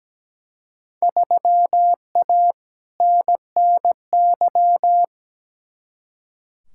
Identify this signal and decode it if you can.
Morse code, heard as a keyed tone, about 17 wpm, 700 Hz. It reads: 3A NNY
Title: Field Day Exchanges - 17wpm